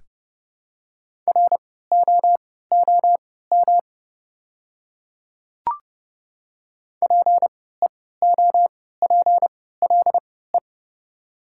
Morse code, heard as a keyed tone, about 30 wpm, spaced as for 17 wpm, 690 Hz. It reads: ROOM PEOPLE
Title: 3x Character Spacing - Top 500 Words - 30wpm